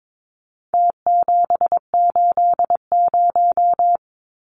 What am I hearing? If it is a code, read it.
Morse code, heard as a keyed tone, about 22 wpm, 700 Hz. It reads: T780
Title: Numbers - Contest Style - 4-Digit 22wpm